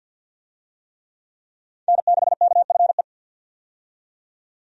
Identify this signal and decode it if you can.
Morse code, heard as a keyed tone, about 51 wpm, 700 Hz. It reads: N6KFE
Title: USA Call Signs - 5 Characters - 50wpm